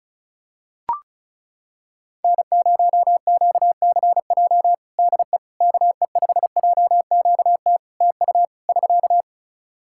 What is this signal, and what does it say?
Morse code, heard as a keyed tone, about 34 wpm, 700 Hz. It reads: N0QCJ DE KE5JQT TU <SK>
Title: Practice QSO - Exchange 4 - Ending - 35wpm